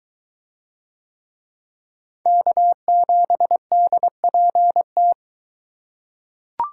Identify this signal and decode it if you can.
Morse code, heard as a keyed tone, about 23 wpm, 700 Hz. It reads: K7DPT